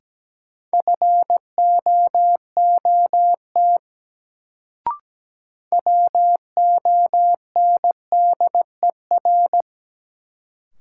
Morse code, heard as a keyed tone, about 17 wpm, 700 Hz. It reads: FOOT WONDER
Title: Sentences from Top 500 Words - New Word Drill - 17wpm